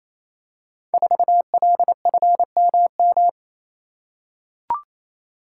Morse code, heard as a keyed tone, about 27 wpm, 700 Hz. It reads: 4LFMM
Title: Code Groups - 5 Letter:Numbers - 28wpm